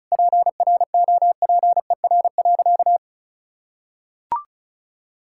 Morse code, heard as a keyed tone, about 36 wpm, 700 Hz. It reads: PROPER.
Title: Sentences from Top 1000 Words - Review Part 5 - 35wpm